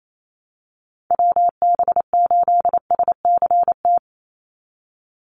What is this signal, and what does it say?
Morse code, heard as a keyed tone, about 28 wpm, 700 Hz. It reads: WB8SCT
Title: Call Signs - 28wpm